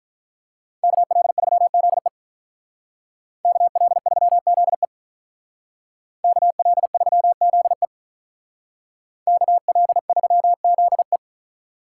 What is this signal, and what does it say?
Morse code, heard as a keyed tone, about 44 wpm, 700 Hz. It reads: KL3ZE KL3ZE KL3ZE KL3ZE
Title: Speed Racing - USA 2x2 Call Signs 35wpm